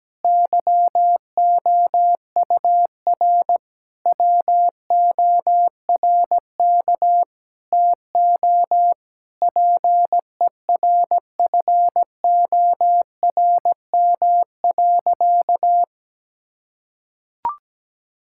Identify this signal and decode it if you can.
Morse code, heard as a keyed tone, about 17 wpm, 700 Hz. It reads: YOUR WORK TO PERFORM.
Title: Sentences from Top 1000 Words - Review Part 5 - 17wpm